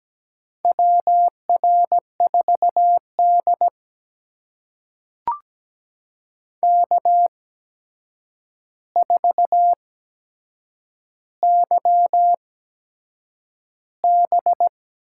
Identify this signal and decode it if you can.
Morse code, heard as a keyed tone, about 17 wpm, spaced as for 7 wpm, 700 Hz. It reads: WR4D K4YB
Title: USA Call Signs - 4 Characters - 8x Character Spacing - 17wpm